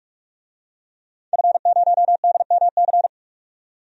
Morse code, heard as a keyed tone, about 44 wpm, 700 Hz. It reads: F0DMC
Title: DXCC-100 Call Signs - 5 Characters - 45wpm